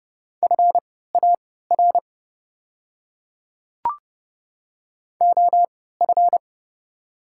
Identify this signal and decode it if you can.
Morse code, heard as a keyed tone, about 30 wpm, spaced as for 17 wpm, 700 Hz. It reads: FAR OF
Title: 3x Character Spacing - Top 300 Words - 30wpm